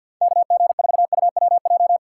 Morse code, heard as a keyed tone, about 52 wpm, 700 Hz. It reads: KG4UWJ
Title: USA Call Signs - 6 Characters - 3x Character Spacing - 50wpm